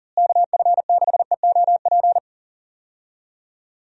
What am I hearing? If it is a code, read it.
Morse code, heard as a keyed tone, about 40 wpm, 700 Hz. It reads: KF6EOP
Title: USA Call Signs - 6 Characters - Repeated 3x - 40wpm